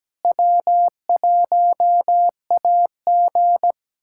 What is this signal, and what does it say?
Morse code, heard as a keyed tone, about 17 wpm, 700 Hz. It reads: W1AG
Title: USA Callsign DE Callsign - 4 Characters - 17wpm